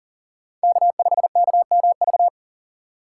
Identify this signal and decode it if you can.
Morse code, heard as a keyed tone, about 40 wpm, 700 Hz. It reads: K5KMV
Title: USA Call Signs - 5 Characters - 6x Character Spacing - 40wpm